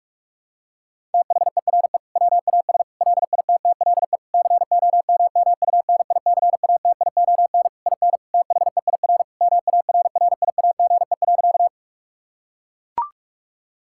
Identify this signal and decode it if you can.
Morse code, heard as a keyed tone, about 41 wpm, 700 Hz. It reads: THERE WAS LITTLE COMMUNICATION IN THEIR MARRIAGE.
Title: Sentences from Top 1000 Words - Review Part 6 - 45wpm